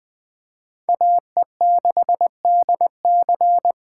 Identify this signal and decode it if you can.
Morse code, heard as a keyed tone, about 23 wpm, 700 Hz. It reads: AE6DC
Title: Speed Racing - USA 2x2 Call Signs 20wpm